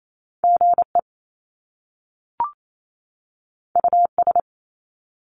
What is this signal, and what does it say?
Morse code, heard as a keyed tone, about 28 wpm, 700 Hz. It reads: GE US
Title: Top Two Letter Combinations in Words 28wpm